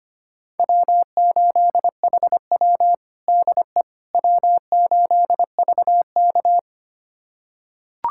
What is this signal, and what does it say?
Morse code, heard as a keyed tone, about 25 wpm, 700 Hz. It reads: W8HW DE W8VK
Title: USA Callsign DE Callsign - 4 Characters - 25wpm